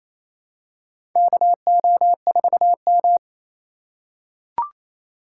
Morse code, heard as a keyed tone, about 28 wpm, 700 Hz. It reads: KO4M
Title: USA Call Signs - 4 Characters - 2x Character Spacing - 28wpm